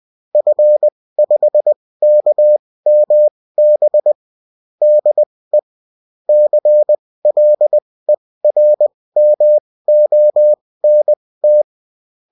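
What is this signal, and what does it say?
Morse code, heard as a keyed tone, about 20 wpm, spaced as for 16 wpm, 600 Hz. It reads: F5KMB DE CLERMONT
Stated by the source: Morse code bulletin of a French amateur CW union, sent as text and in CW.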